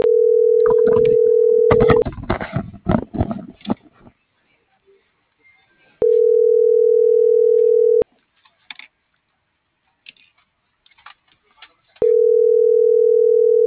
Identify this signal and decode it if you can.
Ringback tone (North America), 440 + 480 Hz.